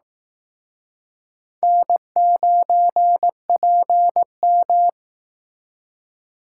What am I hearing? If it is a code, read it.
Morse code, heard as a keyed tone, about 18 wpm, 700 Hz. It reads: N9PM
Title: Speed Racing - USA 1x2 Call Signs 15wpm